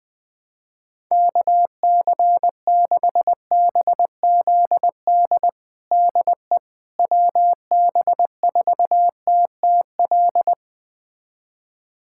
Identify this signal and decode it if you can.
Morse code, heard as a keyed tone, about 20 wpm, 700 Hz. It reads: KC6BZD DE WB4TTL
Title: USA Callsign DE Callsign - 6 Characters - 20wpm